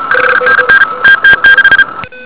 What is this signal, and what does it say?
Morse code, heard as a keyed tone, about 38 wpm, 1600 Hz, with noise. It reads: 5NN TT6
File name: ARCHIV / 80m cw